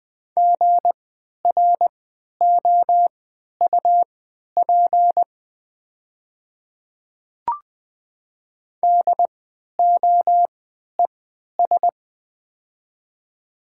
Morse code, heard as a keyed tone, about 20 wpm, spaced as for 11 wpm, 700 Hz. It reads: GROUP DOES
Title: 3x Character Spacing - Top 300 Words - 20wpm